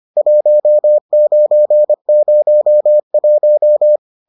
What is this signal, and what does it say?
Morse code, heard as a keyed tone, about 25 wpm, 600 Hz. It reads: 1901